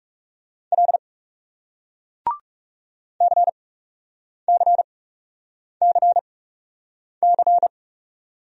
Morse code, heard as a keyed tone, about 41 wpm, spaced as for 8 wpm, 700 Hz. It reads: L CCCC